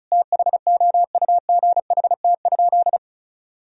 Morse code, heard as a keyed tone, about 36 wpm, 700 Hz. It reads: THOUGHT?
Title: Sentences from Top 300 Words and Abbreviations 35wpm